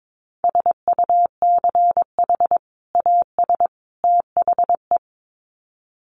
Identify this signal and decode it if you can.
Morse code, heard as a keyed tone, about 22 wpm, 700 Hz. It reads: SUCH AS THE
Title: Sets of 3 Words 22wpm